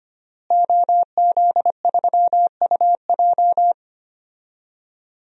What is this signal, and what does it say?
Morse code, heard as a keyed tone, about 25 wpm, 700 Hz. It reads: OZ3UJ
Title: DXCC-100 Call Signs - 5 Characters - Repeated 2x - 25wpm